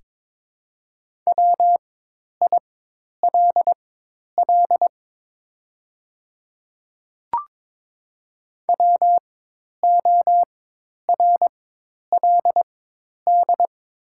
Morse code, heard as a keyed tone, about 22 wpm, spaced as for 10 wpm, 700 Hz. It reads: WILL WORLD
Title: 4x Character Spacing - Top 300 Words - 22wpm